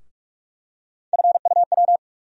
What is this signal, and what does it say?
Morse code, heard as a keyed tone, about 42 wpm, 700 Hz. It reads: FUW
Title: USA Call Sign Postfixes - 3 Letters 45wpm